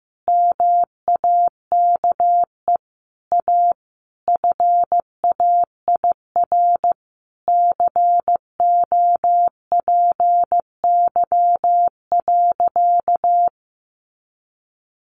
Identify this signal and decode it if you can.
Morse code, heard as a keyed tone, about 15 wpm, 700 Hz. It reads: MAKE A FAIR COPY.
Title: Sentences from Top 1400 Words 15wpm